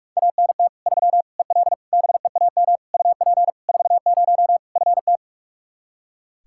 Morse code, heard as a keyed tone, about 41 wpm, 700 Hz. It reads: ANT 3 EL BEAM UP 40 FT
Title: Practice QSO - Exchange 2 - Info - 45wpm